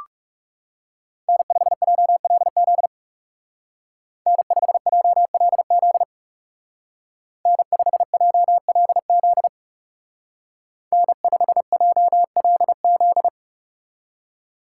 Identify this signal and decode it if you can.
Morse code, heard as a keyed tone, about 36 wpm, 700 Hz. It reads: N5JLZ N5JLZ N5JLZ N5JLZ
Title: Speed Racing - USA 1x3 Call Signs 30wpm